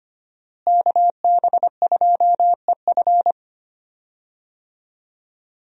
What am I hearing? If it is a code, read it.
Morse code, heard as a keyed tone, about 25 wpm, 700 Hz. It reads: KB2EF
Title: USA Call Signs - 5 Characters - Repeated 2x - 25wpm